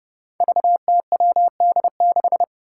Morse code, heard as a keyed tone, about 32 wpm, 700 Hz. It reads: VTWD6
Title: Five Challenging Letter-Number Combinations 30wpm